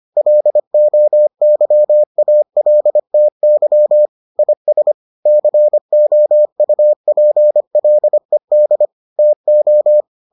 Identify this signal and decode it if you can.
Morse code, heard as a keyed tone, about 25 wpm, 600 Hz. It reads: LOYALTY IS COUPLED TO